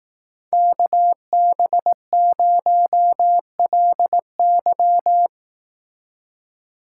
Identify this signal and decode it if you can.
Morse code, heard as a keyed tone, about 18 wpm, 700 Hz. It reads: KB0LY
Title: Speed Racing - USA 2x2 Call Signs 15wpm